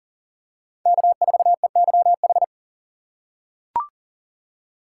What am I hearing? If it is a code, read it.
Morse code, heard as a keyed tone, about 40 wpm, 700 Hz. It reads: K4EYH
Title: USA Call Signs - 5 Characters - 3x Character Spacing - 40wpm